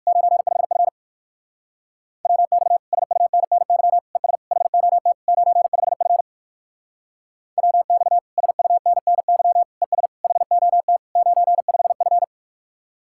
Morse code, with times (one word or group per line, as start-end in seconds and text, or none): 0.1-0.9 95F
2.3-2.8 WX
2.9-4.0 SUNNY
4.2-4.4 ES
4.5-5.1 HOT
5.3-6.2 95F
7.6-8.2 WX
8.4-9.6 SUNNY
9.8-10.1 ES
10.2-11.0 HOT
11.2-12.2 95F